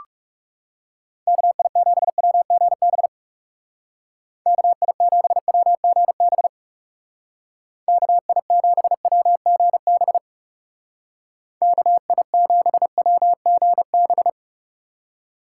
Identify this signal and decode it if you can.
Morse code, heard as a keyed tone, about 38 wpm, 700 Hz. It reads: KI7WGB KI7WGB KI7WGB KI7WGB